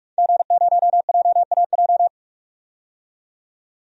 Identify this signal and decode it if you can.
Morse code, heard as a keyed tone, about 43 wpm, 700 Hz. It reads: G0JAJ
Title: DXCC-100 Call Signs - 5 Characters - 45wpm